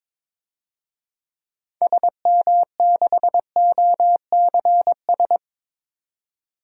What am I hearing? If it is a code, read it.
Morse code, heard as a keyed tone, about 22 wpm, 700 Hz. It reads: SM6OCS
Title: DXCC-100 Call Signs - 6 Characters - 22wpm